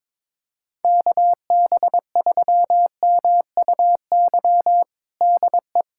Morse code, heard as a keyed tone, about 22 wpm, 700 Hz. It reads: KB3MUY DE
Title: USA Callsign DE Callsign - 6 Characters - 22wpm